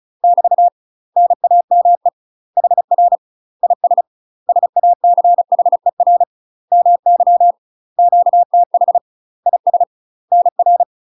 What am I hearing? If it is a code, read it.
Morse code, heard as a keyed tone, about 37 wpm, 700 Hz. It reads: <BT> NAME HR IS SACHER MY QTH IS NR
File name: UK MASTER QSOs TRAINING / 35WPM / UK MASTER QSOs 0008